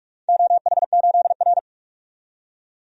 Morse code, heard as a keyed tone, about 44 wpm, 700 Hz. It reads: OH8R
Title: DXCC-100 Call Signs - 4 Characters - 45wpm